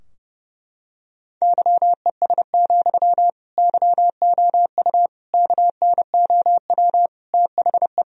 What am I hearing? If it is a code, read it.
Morse code, heard as a keyed tone, about 30 wpm, 700 Hz. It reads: YES, YOU KNOW THE
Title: Sentences from Top 1000 Words - Review Part 4 - 30wpm